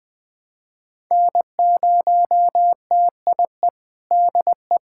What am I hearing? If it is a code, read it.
Morse code, heard as a keyed tone, about 20 wpm, 700 Hz. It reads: N0TIE DE
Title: USA Callsign DE Callsign - 5 Characters - 20wpm